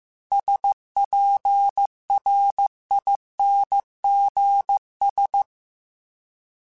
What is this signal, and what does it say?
Morse code, heard as a keyed tone, about 15 wpm, 800 Hz. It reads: SPRINGS